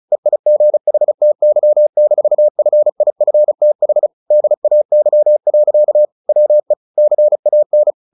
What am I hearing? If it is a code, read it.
Morse code, heard as a keyed tone, about 34 wpm, 600 Hz. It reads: EIGHTY-FIFTH DAY. WE CAN
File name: the old man and the sea / the old man and the sea 1 35wpm